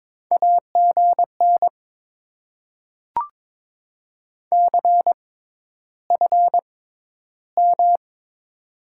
Morse code, heard as a keyed tone, about 22 wpm, spaced as for 11 wpm, 700 Hz. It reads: AGN CFM
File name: Top 90 QSO Elements - 6x Character Spacing - 22wpm